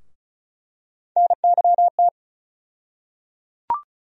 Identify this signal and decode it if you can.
Morse code, heard as a keyed tone, about 34 wpm, 700 Hz. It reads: NYT
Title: USA Call Sign Postfixes - 3 Letters 35wpm